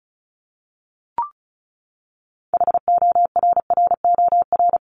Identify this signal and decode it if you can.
Morse code, heard as a keyed tone, about 34 wpm, 700 Hz. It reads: HORROR
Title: Top 1501-1750 Words 35wpm